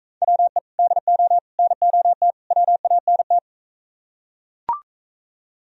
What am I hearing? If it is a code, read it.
Morse code, heard as a keyed tone, about 40 wpm, 700 Hz. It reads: WE DO NOT WANT